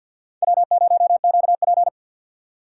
Morse code, heard as a keyed tone, about 47 wpm, 700 Hz. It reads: W0QP
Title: USA Call Signs - 4 Characters - 6x Character Spacing - 50wpm